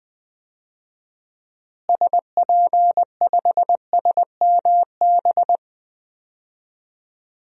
Morse code, heard as a keyed tone, about 20 wpm, 700 Hz. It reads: SP5SMB
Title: DXCC-100 Call Signs - 6 Characters - Repeated 2x - 20wpm